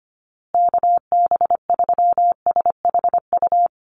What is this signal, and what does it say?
Morse code, heard as a keyed tone, about 25 wpm, 700 Hz. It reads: KB3SHU
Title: USA Call Signs Repeated 3x 25wpm